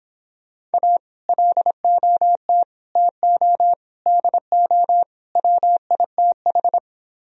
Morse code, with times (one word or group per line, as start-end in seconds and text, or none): 0.7-1.0 A
1.3-2.6 LOT
3.0-3.7 TO
4.1-5.0 DO
5.4-6.8 WITH